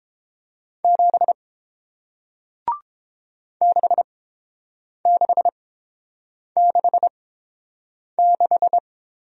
Morse code, heard as a keyed tone, about 30 wpm, spaced as for 7 wpm, 700 Hz. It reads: 7 6666